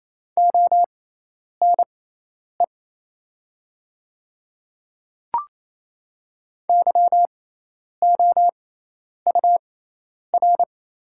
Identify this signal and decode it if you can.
Morse code, heard as a keyed tone, about 28 wpm, spaced as for 10 wpm, 700 Hz. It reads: ONE YOUR
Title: 6x Character Spacing - Top 100 Words - 28wpm